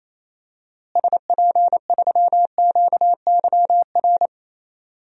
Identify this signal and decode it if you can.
Morse code, heard as a keyed tone, about 28 wpm, 700 Hz. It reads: SP3QYR
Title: DXCC-100 Call Signs - 6 Characters - Repeated 3x - 28wpm